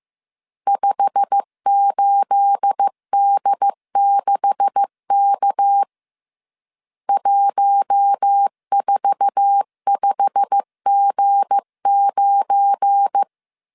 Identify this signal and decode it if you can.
Morse code, heard as a keyed tone, about 15 wpm, 800 Hz. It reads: 58D6K 145G9
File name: CW / DET / 1 9 1500C S C 14wpm 3245